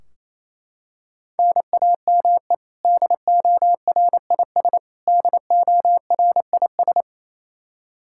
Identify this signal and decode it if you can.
Morse code, heard as a keyed tone, about 28 wpm, 700 Hz. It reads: NAME DORIS DORIS
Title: Practice QSO - Exchange 1 - Beginning - 28wpm